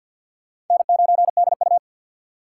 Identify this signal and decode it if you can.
Morse code, heard as a keyed tone, about 52 wpm, 700 Hz. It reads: N9DU
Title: USA Call Signs - 4 Characters - 3x Character Spacing - 50wpm